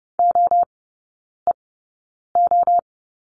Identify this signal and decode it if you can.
Morse code, heard as a keyed tone, about 30 wpm, spaced as for 9 wpm, 700 Hz. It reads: OEO
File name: Lesson 16e - Letter c - Three Characters 30wpm